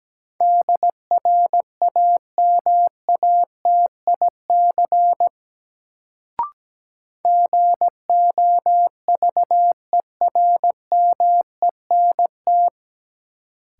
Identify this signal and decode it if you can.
Morse code, heard as a keyed tone, about 17 wpm, 700 Hz. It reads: DRAMATIC GOVERMENT
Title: Sentences from Top 1300 Words - New Word Drill - 17wpm